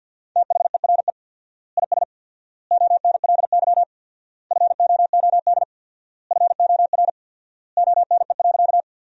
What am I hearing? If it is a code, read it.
Morse code, heard as a keyed tone, about 50 wpm, 700 Hz. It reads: THERE IS ONLY FOOD FOR ONE.
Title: 4x Word-Spacing - Sentences from Top 300 Words - 50wpm